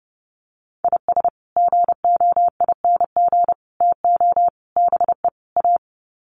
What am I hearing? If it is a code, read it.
Morse code, heard as a keyed tone, about 30 wpm, 700 Hz. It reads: IS GOING TO BE A
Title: Sets of 5 Words 30wpm